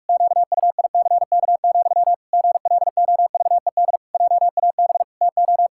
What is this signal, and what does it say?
Morse code, heard as a keyed tone, about 46 wpm, 700 Hz. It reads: QUICK, GLOVED JAB TO
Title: Pangram Challenge v2 - 45wpm